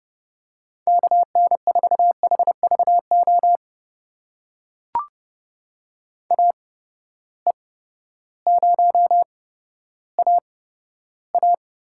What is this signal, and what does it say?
Morse code, heard as a keyed tone, about 30 wpm, spaced as for 14 wpm, 700 Hz. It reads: KN4HVO AE0AA
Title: USA Call Signs - 8x Character Spacing - 30wpm